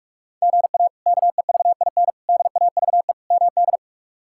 Morse code, heard as a keyed tone, about 46 wpm, 700 Hz. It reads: GA KEVIN DAVE MD
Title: K1USN SST QSO Practice - 45wpm